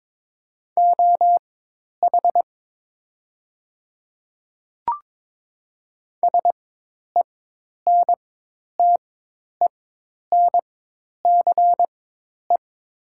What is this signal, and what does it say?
Morse code, heard as a keyed tone, about 22 wpm, spaced as for 10 wpm, 700 Hz. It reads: OH SENTENCE
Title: 4x Character Spacing - Top 500 Words - 22wpm